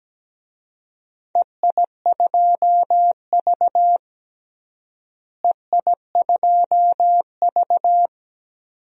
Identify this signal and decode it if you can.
Morse code, heard as a keyed tone, about 17 wpm, 700 Hz. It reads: EI2V EI2V
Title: DXCC-100 Call Signs - 4 Characters - Repeated 2x - 17wpm